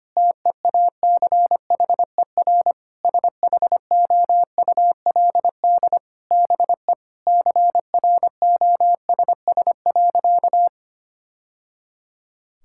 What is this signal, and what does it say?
Morse code, heard as a keyed tone, about 25 wpm, 700 Hz. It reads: TEACHER SHOULD BE CROSS.
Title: Sentences from Top 1000 Words - Review Part 6 - 25wpm